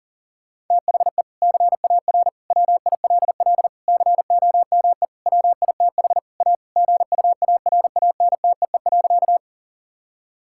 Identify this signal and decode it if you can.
Morse code, heard as a keyed tone, about 37 wpm, 700 Hz. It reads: THE CAR WILL COME WITH A GUARANTEE.
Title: Sentences from Top 1600 Words 40wpm